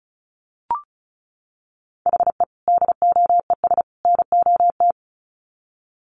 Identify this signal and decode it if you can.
Morse code, heard as a keyed tone, about 34 wpm, 700 Hz. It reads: HE DOES NOT
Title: Sets of 3 Words 35wpm